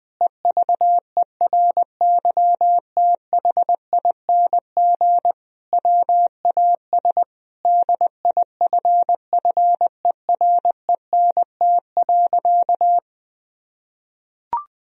Morse code, heard as a keyed tone, about 20 wpm, 700 Hz. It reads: EVERYTHING WAS DIFFERENT.
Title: Sentences from Top 1000 Words - Review Part 3 - 20wpm